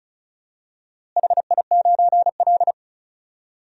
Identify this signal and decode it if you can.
Morse code, heard as a keyed tone, about 34 wpm, 700 Hz. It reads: HI9L
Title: DXCC-100 Call Signs - 4 Characters - Repeated 2x - 35wpm